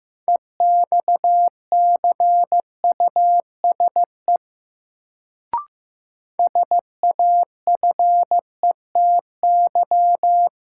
Morse code, heard as a keyed tone, about 15 wpm, 700 Hz. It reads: EXCUSE SAFETY
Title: Sentences from Top 700 Words - New Word Drill - 15wpm